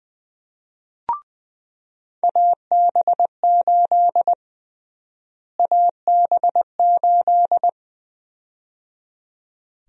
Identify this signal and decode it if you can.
Morse code, heard as a keyed tone, about 20 wpm, 700 Hz. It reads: AB8 AB8
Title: USA Call Sign Prefixes - Repeated 2x - 20wpm